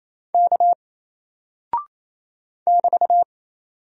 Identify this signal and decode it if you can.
Morse code, heard as a keyed tone, about 27 wpm, 700 Hz. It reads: K <BT>
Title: Single Character (Letter:Number:Punctuation:Prosign) 28wpm